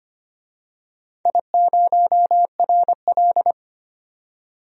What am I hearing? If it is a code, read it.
Morse code, heard as a keyed tone, about 25 wpm, 700 Hz. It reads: I0RL
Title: DXCC-100 Call Signs - 4 Characters - 25wpm